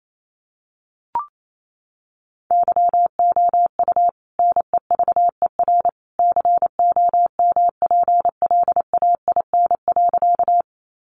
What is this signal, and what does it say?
Morse code, heard as a keyed tone, about 28 wpm, 700 Hz. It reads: YOU NEVER COMPLAIN.
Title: Sentences from Top 1300 Words 28wpm